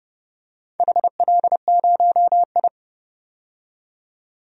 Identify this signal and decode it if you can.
Morse code, heard as a keyed tone, about 32 wpm, 700 Hz. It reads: HL0I
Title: DXCC-100 Call Signs - 4 Characters - 30wpm